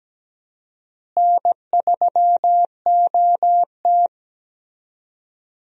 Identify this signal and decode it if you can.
Morse code, heard as a keyed tone, about 17 wpm, 700 Hz. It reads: N3OT
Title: USA Call Signs - 4 Characters - Repeated 3x - 17wpm